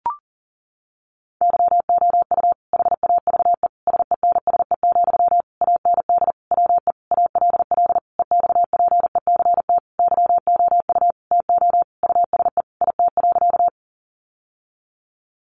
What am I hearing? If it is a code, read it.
Morse code, heard as a keyed tone, about 40 wpm, 700 Hz. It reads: YOU HAVE SENSE, AND WE ALL EXPECT YOU TO USE IT.
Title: Sentences from Top 1100 Words 40wpm